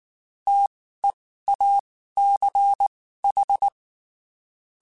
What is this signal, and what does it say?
Morse code, heard as a keyed tone, about 19 wpm, spaced as for 14 wpm, 800 Hz. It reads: TEACH
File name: QSOs / QSOs-mp3s / WD0BGP 1@18